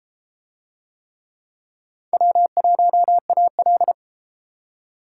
Morse code, heard as a keyed tone, about 31 wpm, 700 Hz. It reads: W1AL